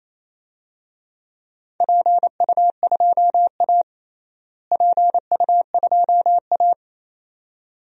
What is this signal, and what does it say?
Morse code, heard as a keyed tone, about 28 wpm, 700 Hz. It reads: PU2A PU2A
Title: DXCC-100 Call Signs - 4 Characters - Repeated 2x - 28wpm